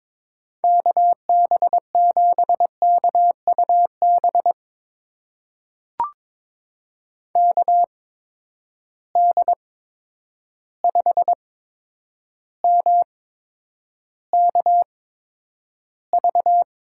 Morse code, heard as a keyed tone, about 22 wpm, spaced as for 9 wpm, 700 Hz. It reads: KB7KUB KD5MKV
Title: USA Call Signs - 6 Characters - 8x Character Spacing - 22wpm